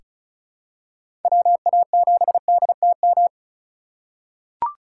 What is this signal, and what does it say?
Morse code, heard as a keyed tone, about 36 wpm, 700 Hz. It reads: WA7DTM
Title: USA Call Signs - 2x Character Spacing - 35wpm